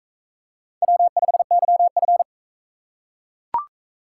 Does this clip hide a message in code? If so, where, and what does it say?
Yes: Morse code from 0.8 to 2.2 s, heard as a keyed tone, about 41 wpm, 700 Hz. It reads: W5YF